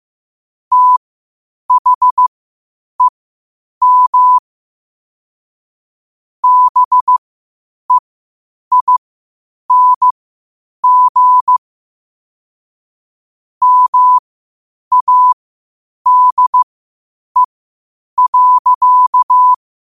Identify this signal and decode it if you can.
Morse code, heard as a keyed tone, about 15 wpm, spaced as for 8 wpm, 1000 Hz. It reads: THEM BEING MADE.
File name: qotdfiles / 8wpm